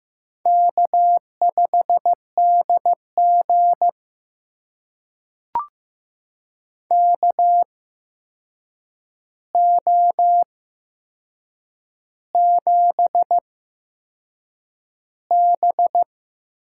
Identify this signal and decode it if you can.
Morse code, heard as a keyed tone, about 15 wpm, spaced as for 6 wpm, 700 Hz. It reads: K5DG KO7B
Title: USA Call Signs - 4 Characters - 8x Character Spacing - 15wpm